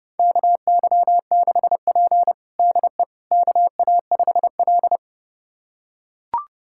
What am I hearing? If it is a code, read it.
Morse code, heard as a keyed tone, about 32 wpm, 700 Hz. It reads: KY6P DE KA5L
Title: USA Callsign DE Callsign - 4 Characters - 30wpm